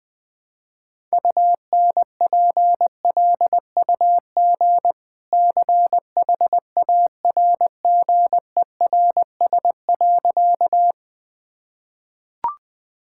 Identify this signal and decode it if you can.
Morse code, heard as a keyed tone, about 20 wpm, 700 Hz. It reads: UNPLUG CHARGERS.
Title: Easy Sentences with Hints - 20wpm